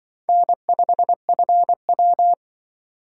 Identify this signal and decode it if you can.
Morse code, heard as a keyed tone, about 25 wpm, 700 Hz. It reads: N5FW
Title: Speed Racing - USA 1x2 Call Signs 20wpm